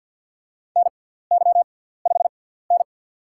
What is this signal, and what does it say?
Morse code, heard as a keyed tone, about 47 wpm, spaced as for 17 wpm, 700 Hz. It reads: NY5N
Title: USA Call Signs - 4 Characters - 6x Character Spacing - 50wpm